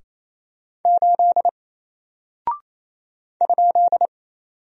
Morse code, heard as a keyed tone, about 27 wpm, 700 Hz. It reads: 8 ?
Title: Single Character (Letter:Number:Punctuation:Prosign) 28wpm